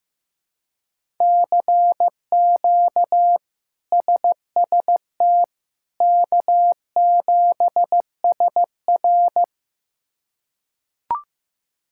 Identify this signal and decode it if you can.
Morse code, heard as a keyed tone, about 16 wpm, 700 Hz. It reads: CQ SST K7SR
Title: K1USN SST QSO Practice - 15wpm